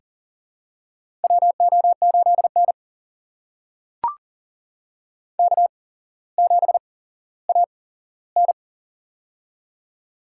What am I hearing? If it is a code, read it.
Morse code, heard as a keyed tone, about 38 wpm, spaced as for 17 wpm, 700 Hz. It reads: WO8N K7AN
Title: USA Call Signs - 4 Characters - 8x Character Spacing - 40wpm